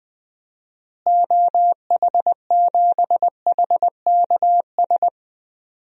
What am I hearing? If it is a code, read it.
Morse code, heard as a keyed tone, about 21 wpm, 700 Hz. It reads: OH7HKS
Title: DXCC-100 Call Signs - 6 Characters - Repeated 3x - 20wpm